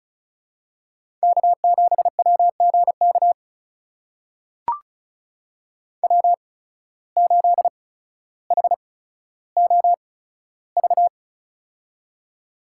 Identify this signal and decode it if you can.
Morse code, heard as a keyed tone, about 36 wpm, spaced as for 15 wpm, 700 Hz. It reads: K7WGK W8HOV
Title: USA Call Signs - 5 Characters - 8x Character Spacing - 35wpm